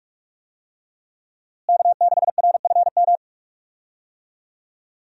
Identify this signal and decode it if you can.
Morse code, heard as a keyed tone, about 46 wpm, 700 Hz. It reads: K6RFM
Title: USA Call Signs - 5 Characters - Repeated 2x - 45wpm